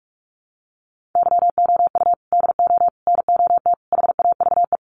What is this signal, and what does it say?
Morse code, heard as a keyed tone, about 44 wpm, 700 Hz. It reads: YOU DO NOT HAVE
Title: Sets of 5 Words 45wpm